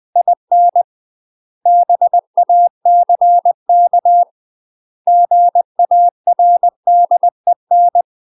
Morse code, heard as a keyed tone, about 20 wpm, 700 Hz. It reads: IN BACK GARDEN
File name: UK MASTER QSOs TRAINING/20WPM